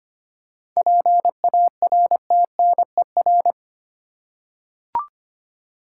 Morse code, heard as a keyed tone, about 25 wpm, 700 Hz. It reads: PARTNER
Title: Seven Letter Words - 25wpm